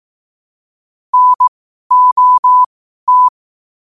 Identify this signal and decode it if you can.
Morse code, heard as a keyed tone, about 18 wpm, spaced as for 12 wpm, 1000 Hz. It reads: NOT
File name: qotdfiles / 12wpm